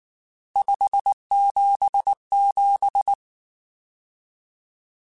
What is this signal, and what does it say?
Morse code, heard as a keyed tone, about 19 wpm, 800 Hz. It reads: 577
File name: RSTs / RST-mp3s / Various RST @18